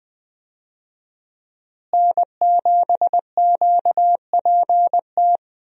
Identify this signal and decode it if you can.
Morse code, heard as a keyed tone, about 20 wpm, 700 Hz. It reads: N7QPT
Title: USA Call Signs - 5 Characters - Repeated 3x - 20wpm